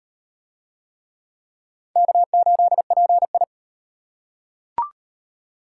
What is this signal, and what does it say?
Morse code, heard as a keyed tone, about 36 wpm, 700 Hz. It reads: K8PI